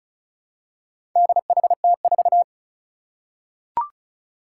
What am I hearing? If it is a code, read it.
Morse code, heard as a keyed tone, about 36 wpm, 700 Hz. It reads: DHT4
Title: Four Challenging Letter-Number Combinations 35wpm